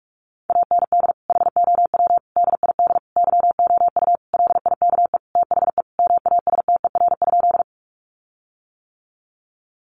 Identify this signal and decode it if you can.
Morse code, heard as a keyed tone, about 44 wpm, 700 Hz. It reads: AND HOW DID YOU LIKE THE MASTER?
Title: Sentences from Top 1300 Words 45wpm